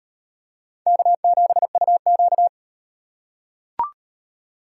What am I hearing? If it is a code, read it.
Morse code, heard as a keyed tone, about 38 wpm, 700 Hz. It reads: K7UQ